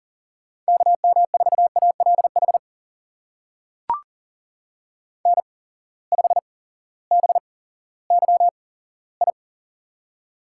Morse code, heard as a keyed tone, about 39 wpm, spaced as for 18 wpm, 700 Hz. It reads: KM4ALH N5BYI
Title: USA Call Signs - 8x Character Spacing - 40wpm